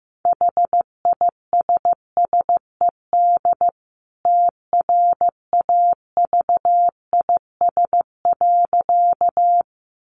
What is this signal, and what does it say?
Morse code, heard as a keyed tone, about 15 wpm, 700 Hz. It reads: HISSED TRAVIS.
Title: A Sound Of Thunder 15wpm